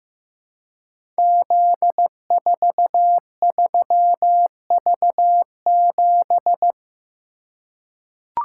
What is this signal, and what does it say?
Morse code, heard as a keyed tone, about 15 wpm, 700 Hz. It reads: Z43V7
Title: Code Groups - 5 Letter:Numbers - 15wpm